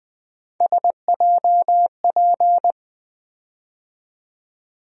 Morse code, heard as a keyed tone, about 20 wpm, 700 Hz. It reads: SJP
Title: USA Call Sign Postfixes - 3 Letters 20wpm